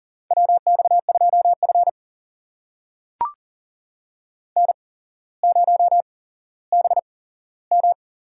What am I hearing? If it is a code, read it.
Morse code, heard as a keyed tone, about 37 wpm, spaced as for 17 wpm, 700 Hz. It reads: WX2F N0BM
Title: USA Call Signs - 4 Characters - 8x Character Spacing - 40wpm